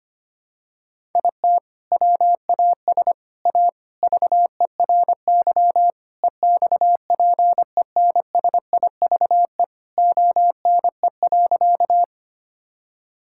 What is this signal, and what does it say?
Morse code, heard as a keyed tone, about 25 wpm, 700 Hz. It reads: IT WAS A VERY EXPENSIVE ONE.
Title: Sentences from Top 1000 Words - Review Part 6 - 25wpm